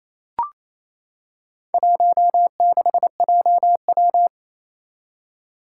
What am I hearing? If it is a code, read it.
Morse code, heard as a keyed tone, about 28 wpm, 700 Hz. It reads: 16JW
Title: Four Challenging Letter-Number Combinations 28wpm